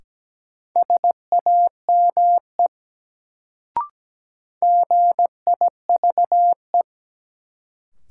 Morse code, heard as a keyed tone, about 17 wpm, 700 Hz. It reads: SAME GIVE
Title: Sentences from Top 200 Words - New Word Drill - 17wpm